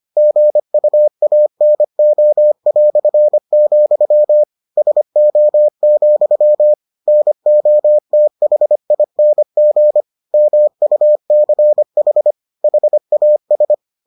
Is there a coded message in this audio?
Yes: Morse code from 0.2 to 13.7 s, heard as a keyed tone, about 25 wpm, 600 Hz. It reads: GUANO", SO, NOTHING MUCH HAS